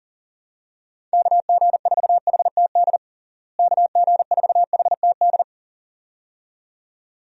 Morse code, heard as a keyed tone, about 40 wpm, 700 Hz. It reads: KG4HTD KG4HTD
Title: USA Call Signs - 6 Characters - Repeated 2x - 40wpm